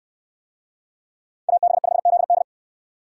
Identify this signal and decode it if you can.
Morse code, heard as a keyed tone, about 69 wpm, 700 Hz. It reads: AB5ZD